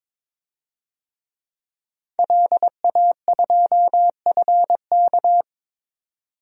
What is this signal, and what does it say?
Morse code, heard as a keyed tone, about 21 wpm, 700 Hz. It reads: LA2FK
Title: DXCC-100 Call Signs - 5 Characters - 22wpm